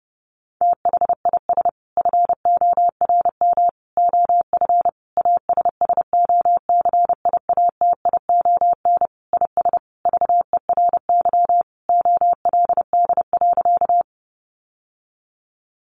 Morse code, heard as a keyed tone, about 30 wpm, 700 Hz. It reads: THIS FORM OF ASSOCIATION IS VERY OLD.
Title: Sentences from Top 1100 Words 30wpm